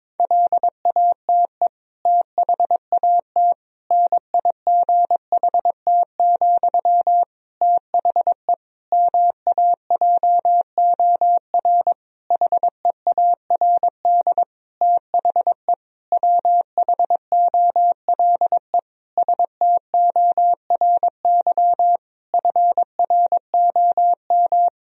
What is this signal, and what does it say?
Morse code, heard as a keyed tone, about 22 wpm, 700 Hz. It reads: LATE THAT NIGHT, THE MAJOR HEARD THE WHOLE STORY FROM
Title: Sentences from Top 1000 Words - Review Part 4 - 22wpm